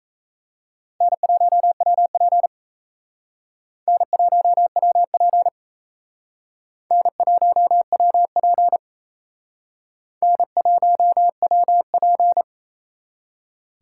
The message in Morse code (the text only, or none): N1WP N1WP N1WP N1WP